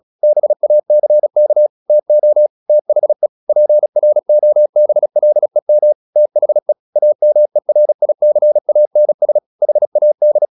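Morse code, heard as a keyed tone, about 36 wpm, 600 Hz. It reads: BACK TO THE PROBLEM THE AMERICANS HAD
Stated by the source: Morse practice at 35 wpm from CW med Gnister, sending Ken Follett's novel Never in English.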